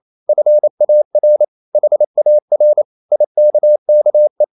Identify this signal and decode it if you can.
Morse code, heard as a keyed tone, about 28 wpm, 600 Hz. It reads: FAR HAR IKKE
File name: Ferie QRQ 28wpm